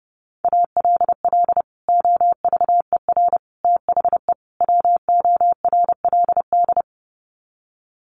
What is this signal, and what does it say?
Morse code, heard as a keyed tone, about 30 wpm, 700 Hz. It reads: ALL OVER THE WORLD
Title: Sets of 4 Words 30wpm